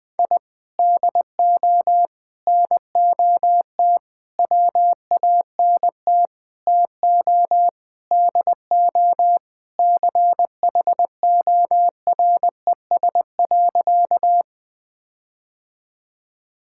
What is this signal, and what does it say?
Morse code, heard as a keyed tone, about 21 wpm, 700 Hz. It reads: I DO NOT WANT TO DO CHORES.
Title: Sentences with 5-Word Phrases - 20wpm